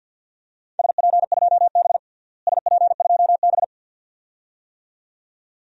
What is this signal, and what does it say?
Morse code, heard as a keyed tone, about 52 wpm, 700 Hz. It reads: SP2B SP2B
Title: DXCC-100 Call Signs - 4 Characters - Repeated 2x - 50wpm